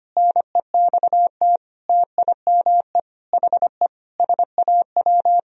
Morse code, heard as a keyed tone, about 25 wpm, 700 Hz. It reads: NEXT TIME HE SAW
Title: Sentences from Top 1000 Words - Review Part 1 - 25wpm